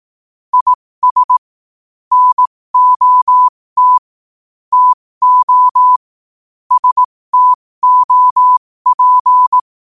Morse code, heard as a keyed tone, about 18 wpm, spaced as for 15 wpm, 1000 Hz. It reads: IS NOT TO STOP
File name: qotdfiles / 15wpm